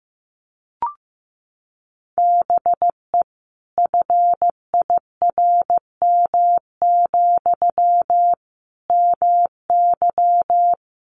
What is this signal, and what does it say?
Morse code, heard as a keyed tone, about 15 wpm, 700 Hz. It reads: BE FIRM, MY
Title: Sentences from Top 1400 Words 15wpm